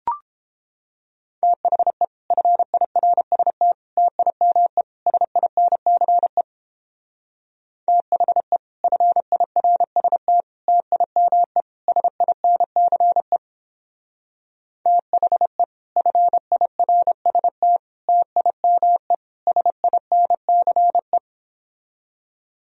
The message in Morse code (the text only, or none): THE FIRST TIME SINCE THE FIRST TIME SINCE THE FIRST TIME SINCE